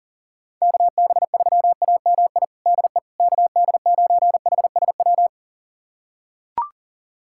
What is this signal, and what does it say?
Morse code, heard as a keyed tone, about 37 wpm, 700 Hz. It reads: KB3AMI DE KD9HSW
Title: USA Callsign DE Callsign - 6 Characters - 40wpm